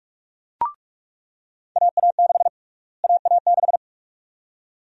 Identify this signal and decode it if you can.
Morse code, heard as a keyed tone, about 46 wpm, 700 Hz. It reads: AA6 AA6
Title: USA Call Sign Prefixes - Repeated 2x - 45wpm